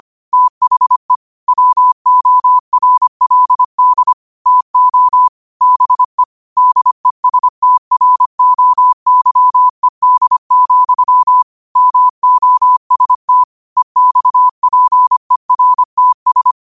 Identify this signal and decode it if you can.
Morse code, heard as a keyed tone, about 25 wpm, 1000 Hz. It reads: THE WORLD TO BE DESTROYED, MOST EXPERTS
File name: qotdfiles / 25wpm